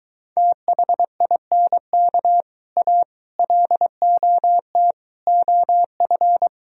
Morse code, heard as a keyed tone, about 23 wpm, 700 Hz. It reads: THINK A LOT OF